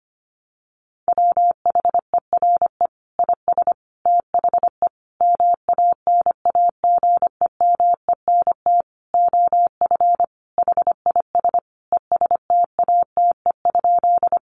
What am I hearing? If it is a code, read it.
Morse code, heard as a keyed tone, about 25 wpm, 700 Hz. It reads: WHERE IS THE MANAGEMENT OF HIS ESTATE?
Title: Sentences from Top 1200 Words 25wpm